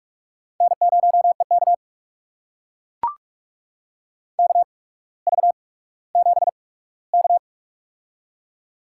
Morse code, heard as a keyed tone, about 44 wpm, spaced as for 19 wpm, 700 Hz. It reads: N0EK KV7K
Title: USA Call Signs - 4 Characters - 8x Character Spacing - 45wpm